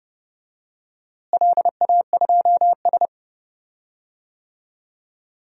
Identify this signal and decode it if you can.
Morse code, heard as a keyed tone, about 30 wpm, 700 Hz. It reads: LA2S
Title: DXCC-100 Call Signs - 4 Characters - Repeated 3x - 30wpm